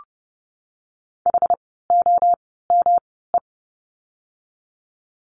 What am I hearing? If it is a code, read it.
Morse code, heard as a keyed tone, about 30 wpm, spaced as for 17 wpm, 700 Hz. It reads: HOME
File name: course / 30:10wpm / 108 Lesson 18f - Letter m - Words 30wpm